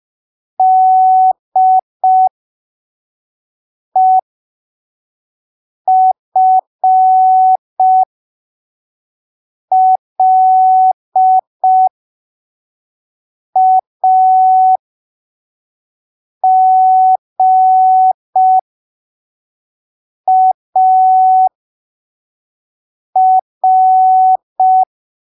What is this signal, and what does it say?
Morse code, heard as a keyed tone, about 5 wpm, spaced as for 3 wpm, 750 Hz. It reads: DEFLAGAR
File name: arquivos / texto / texto 30 05ppm